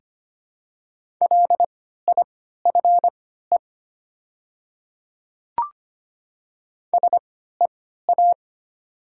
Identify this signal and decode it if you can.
Morse code, heard as a keyed tone, about 25 wpm, spaced as for 14 wpm, 700 Hz. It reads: LIFE SEA
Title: 3x Character Spacing - Top 500 Words - 25wpm